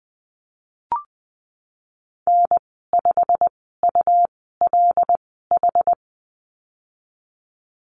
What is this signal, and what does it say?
Morse code, heard as a keyed tone, about 19 wpm, spaced as for 14 wpm, 700 Hz. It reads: N5ULH
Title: 213 Lesson 34g - Number 8 - Call Signs 20wpm